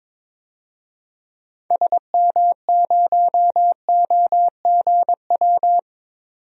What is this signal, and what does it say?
Morse code, heard as a keyed tone, about 22 wpm, 700 Hz. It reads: SM0OGW
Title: DXCC-100 Call Signs - 6 Characters - 22wpm